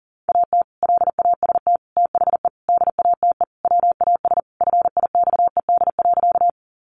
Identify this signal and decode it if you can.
Morse code, heard as a keyed tone, about 40 wpm, 700 Hz. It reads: AT LAST THE DATE WAS FIXED.
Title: Sentences from Top 1100 Words 40wpm